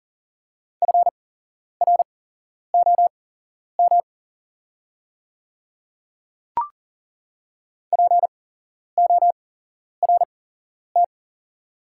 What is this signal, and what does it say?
Morse code, heard as a keyed tone, about 40 wpm, spaced as for 11 wpm, 700 Hz. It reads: FROM PORT
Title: 8x Character Spacing - Top 200 Words - 40wpm